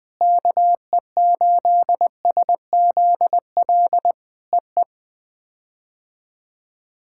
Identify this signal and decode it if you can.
Morse code, heard as a keyed tone, about 21 wpm, 700 Hz. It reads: KE8SZL EE
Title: POTA QSO Practice - 20wpm